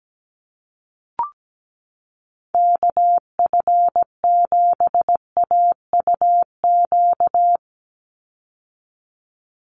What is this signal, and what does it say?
Morse code, heard as a keyed tone, about 17 wpm, 700 Hz. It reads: KF7AUQ
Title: USA Call Signs 17wpm